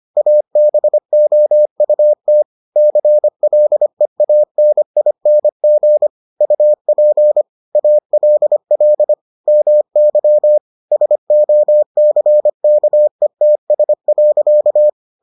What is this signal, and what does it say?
Morse code, heard as a keyed tone, about 25 wpm, 600 Hz. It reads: ABOUT CLEANING UP ALL MY SOCKETS.